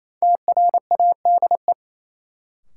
Morse code, heard as a keyed tone, about 28 wpm, 700 Hz. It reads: TRADE
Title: Sets of 5 Words Encore - New Word Drill - 28wpm